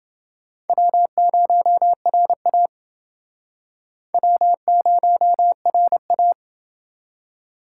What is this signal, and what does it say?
Morse code, heard as a keyed tone, about 28 wpm, 700 Hz. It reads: W0RA W0RA